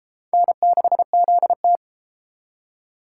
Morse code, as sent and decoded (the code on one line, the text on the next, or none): -. -.... --.. -
N6ZT